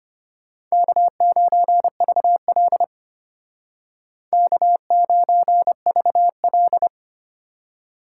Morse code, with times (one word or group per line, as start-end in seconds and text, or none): 0.7-2.8 K9VL
4.3-6.9 K9VL